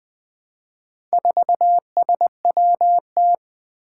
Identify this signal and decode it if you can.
Morse code, heard as a keyed tone, about 20 wpm, 700 Hz. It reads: 4SWT
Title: Four Challenging Letter-Number Combinations 20wpm